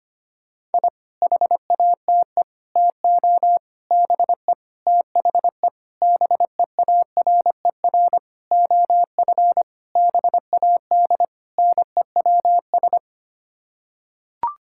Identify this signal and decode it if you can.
Morse code, heard as a keyed tone, about 25 wpm, 700 Hz. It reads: I HATE TO BE THE BEARER OF BAD NEWS
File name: Sets of 9 Words - 25wpm